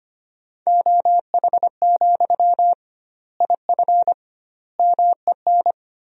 Morse code, heard as a keyed tone, about 25 wpm, 700 Hz. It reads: OH, IF MEN
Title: 2x Word-Spacing - Sentences from Top 500 Words - 25wpm